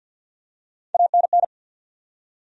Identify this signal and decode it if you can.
Morse code, heard as a keyed tone, about 50 wpm, 700 Hz. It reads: ANN
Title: USA Common Names - Max 3 Letters - 50wpm